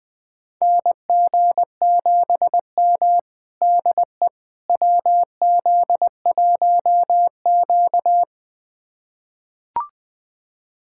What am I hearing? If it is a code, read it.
Morse code, heard as a keyed tone, about 21 wpm, 700 Hz. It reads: NG7M DE WZ1Q
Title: USA Callsign DE Callsign - 4 Characters - 20wpm